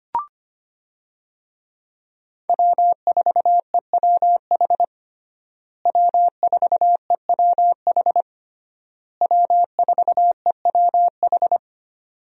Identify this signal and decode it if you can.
Morse code, heard as a keyed tone, about 25 wpm, 700 Hz. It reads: W4EWH W4EWH W4EWH
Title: USA Call Signs - 5 Characters - Repeated 3x - 25wpm